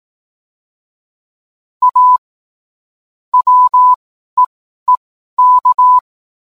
Morse code, heard as a keyed tone, about 18 wpm, spaced as for 12 wpm, 1000 Hz. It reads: A WEEK